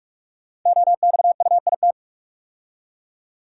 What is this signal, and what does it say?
Morse code, heard as a keyed tone, about 44 wpm, 690 Hz. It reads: OXUIT
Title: Code Groups - 5 Letters - 45wpm